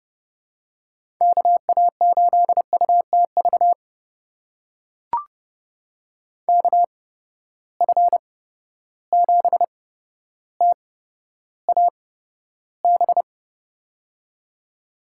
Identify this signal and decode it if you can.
Morse code, heard as a keyed tone, about 30 wpm, spaced as for 13 wpm, 700 Hz. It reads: KA8UTV KF7TAB
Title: USA Call Signs - 6 Characters - 8x Character Spacing - 30wpm